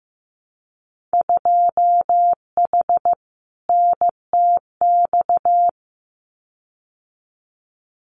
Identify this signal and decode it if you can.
Morse code, heard as a keyed tone, about 15 wpm, 700 Hz. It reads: 2H NTX
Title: Winter Field Day Exchanges Encore 15wpm